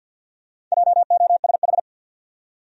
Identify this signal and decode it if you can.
Morse code, heard as a keyed tone, about 53 wpm, 700 Hz. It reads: JOSH
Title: USA Common Names - Max 4 Letters - 50wpm